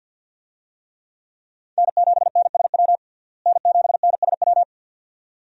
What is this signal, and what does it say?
Morse code, heard as a keyed tone, about 50 wpm, 700 Hz. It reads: N7NSW N7NSW
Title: USA Call Signs - 5 Characters - Repeated 2x - 50wpm